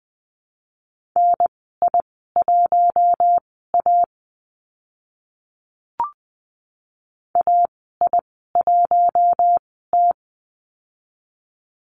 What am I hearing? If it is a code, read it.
Morse code, heard as a keyed tone, about 20 wpm, spaced as for 14 wpm, 700 Hz. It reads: NI1A AI1T
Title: 042 Lesson 8g - Number 1 - Call Signs 20wpm